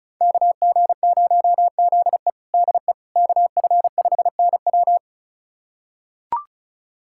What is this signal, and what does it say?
Morse code, heard as a keyed tone, about 34 wpm, 700 Hz. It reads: KG0ZE DE KF5NW
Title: USA Callsign DE Callsign - 5 Characters - 35wpm